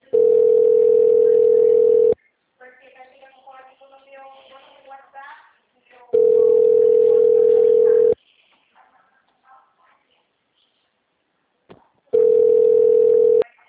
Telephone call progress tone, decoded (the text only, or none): ringback tone